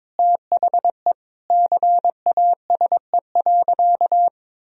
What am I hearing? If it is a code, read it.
Morse code, heard as a keyed tone, about 22 wpm, 700 Hz. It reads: THE CASE.
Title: Sentences from Top 1000 Words - Review Part 3 - 22wpm